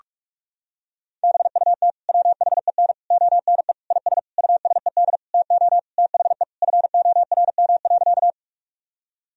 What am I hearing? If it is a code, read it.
Morse code, heard as a keyed tone, about 41 wpm, 700 Hz. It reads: BUT WHEN ONE IS USED TO THE FORM.
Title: Sentences from Top 1000 Words - Review Part 1 - 45wpm